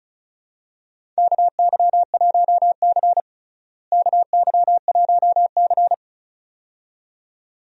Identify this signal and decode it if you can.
Morse code, heard as a keyed tone, about 34 wpm, 700 Hz. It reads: KY1C KY1C
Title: USA Call Signs - 4 Characters - Repeated 2x - 35wpm